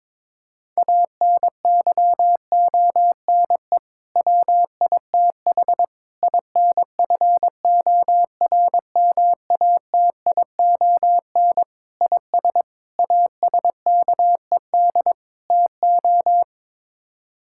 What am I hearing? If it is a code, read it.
Morse code, heard as a keyed tone, about 22 wpm, 700 Hz. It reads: ANYONE WITH INFORMATION IS ASKED TO
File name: practice / 22wpm / Sets of 6 Words - 22wpm